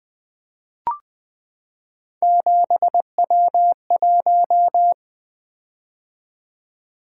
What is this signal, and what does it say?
Morse code, heard as a keyed tone, about 20 wpm, 700 Hz. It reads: 7W1
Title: Three Letter-Number Combinations - 20wpm